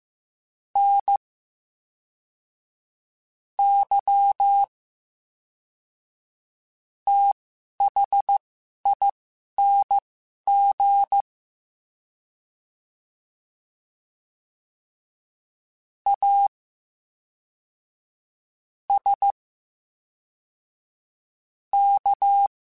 Morse code, heard as a keyed tone, about 15 wpm, spaced as for 5 wpm, 800 Hz. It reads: NYTHING ASK
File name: Special INGs / Special INGs-mp3s / VerySlowToFaster Ing @10